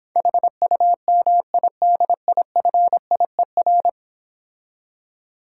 Morse code, heard as a keyed tone, about 26 wpm, 700 Hz. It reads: HUMIDIFIER